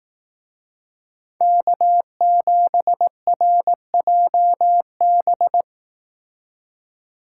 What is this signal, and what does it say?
Morse code, heard as a keyed tone, about 18 wpm, 700 Hz. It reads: K7RJB
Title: Speed Racing - USA 1x3 Call Signs 15wpm